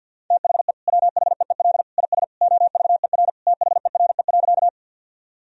Morse code, heard as a keyed tone, about 49 wpm, 700 Hz. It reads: THE WHEEL IS OVER THERE.
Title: Sentences from Top 1000 Words - Review Part 3 - 50wpm